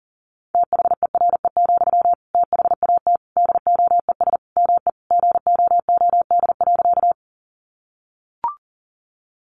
Morse code, heard as a keyed tone, about 40 wpm, 700 Hz. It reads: THERE, THAT DOES ME GOOD.
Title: Sentences from Top 200 Words 40wpm